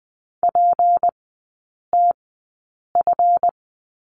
Morse code, heard as a keyed tone, about 20 wpm, spaced as for 8 wpm, 700 Hz. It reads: PTF
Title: 152 Lesson 25e - Letter p - Three Characters 20wpm